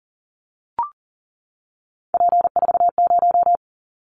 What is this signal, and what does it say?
Morse code, heard as a keyed tone, about 40 wpm, 700 Hz. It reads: P40
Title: Callsign Prefixes for Top 100 DXCC - 40wpm